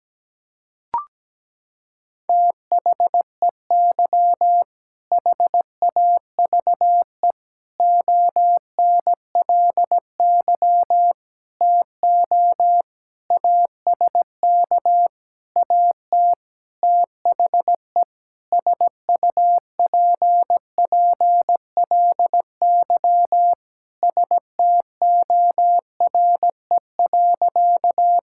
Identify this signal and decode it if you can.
Morse code, heard as a keyed tone, about 17 wpm, 700 Hz. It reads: THEY HAVE ONLY TO ASK AT THE SUPPLY STORE.
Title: Sentences from Top 1000 Words - Review Part 5 - 17wpm